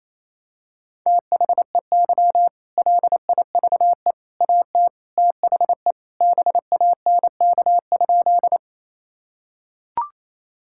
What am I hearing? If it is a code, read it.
Morse code, heard as a keyed tone, about 28 wpm, 700 Hz. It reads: THEY LIVE AT THE BANK?
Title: Sentences from Top 1000 Words - Review Part 6 - 28wpm